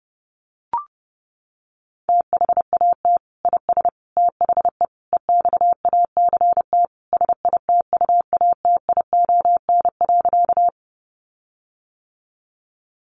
Morse code, heard as a keyed tone, about 28 wpm, 700 Hz. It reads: THAT IS THE EXACT SITUATION.
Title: Sentences from Top 1300 Words 30wpm